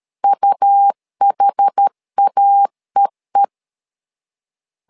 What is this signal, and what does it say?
Morse code, heard as a keyed tone, about 13 wpm, 800 Hz. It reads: UHAEE
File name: LEZIONI / 2 UHDN / 2 UH 12wpm 1439